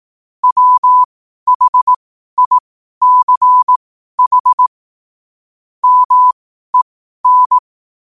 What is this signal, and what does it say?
Morse code, heard as a keyed tone, about 18 wpm, spaced as for 12 wpm, 1000 Hz. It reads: WHICH MEN
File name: qotdfiles / 12wpm